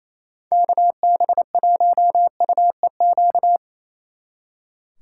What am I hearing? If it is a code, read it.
Morse code, heard as a keyed tone, about 26 wpm, 700 Hz. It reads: KB1UEQ
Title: Speed Racing - USA 2x3 Call Signs 28wpm